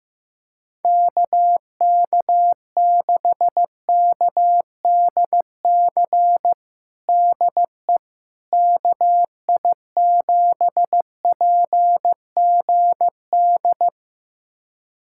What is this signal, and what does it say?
Morse code, heard as a keyed tone, about 15 wpm, 700 Hz. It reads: KK6KDC DE KI7PGD
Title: USA Callsign DE Callsign 15wpm